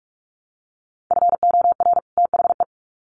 Morse code, heard as a keyed tone, about 46 wpm, 700 Hz. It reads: FOR THE
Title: Sets of 2 Words 45wpm